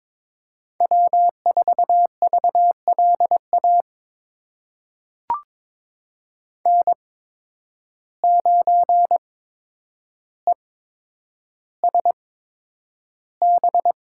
Morse code, heard as a keyed tone, about 22 wpm, spaced as for 9 wpm, 700 Hz. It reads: W4VLA N9ESB
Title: USA Call Signs - 5 Characters - 8x Character Spacing - 22wpm